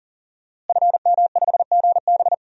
Morse code, heard as a keyed tone, about 40 wpm, 700 Hz. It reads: FM5GB
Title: DXCC-100 Call Signs - 5 Characters - Repeated 3x - 40wpm